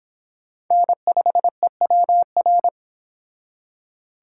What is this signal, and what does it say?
Morse code, heard as a keyed tone, about 28 wpm, 700 Hz. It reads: N5EWR